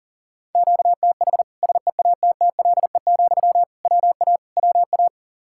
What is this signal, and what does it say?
Morse code, heard as a keyed tone, about 40 wpm, 700 Hz. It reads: QTH SEATTLE, WA WA
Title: Practice QSO - Exchange 1 - Beginning - 40wpm